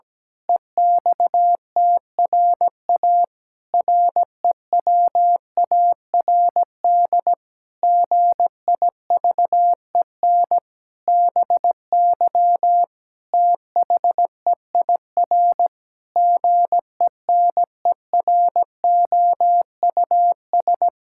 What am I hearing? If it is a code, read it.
Morse code, heard as a keyed tone, about 17 wpm, 700 Hz. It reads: EXTRA REWARD GIVEN BY THEIR GENEROUS
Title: Pangram Challenge v2 - 17wpm